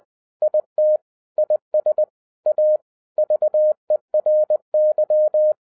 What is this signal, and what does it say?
Morse code, heard as a keyed tone, about 20 wpm, 610 Hz. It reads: IT IS A VERY